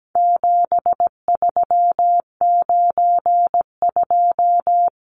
Morse code, heard as a keyed tone, about 17 wpm, 700 Hz. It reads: 7392
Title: Numbers - 4-Digit 17wpm